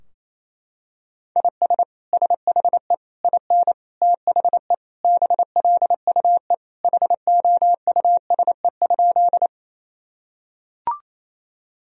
Morse code, heard as a keyed tone, about 28 wpm, 700 Hz. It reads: IS SHE IN THE BLUE HOUSE?
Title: Sentences from Top 1000 Words - Review Part 3 - 28wpm